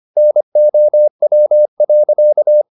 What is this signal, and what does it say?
Morse code, heard as a keyed tone, about 25 wpm, 600 Hz. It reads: NOW.
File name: the old man and the sea part 1 25wpm